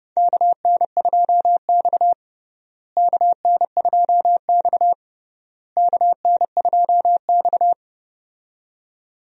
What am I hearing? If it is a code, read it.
Morse code, heard as a keyed tone, about 32 wpm, 700 Hz. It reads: KN2X KN2X KN2X
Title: USA Call Signs - 4 Characters - Repeated 3x - 30wpm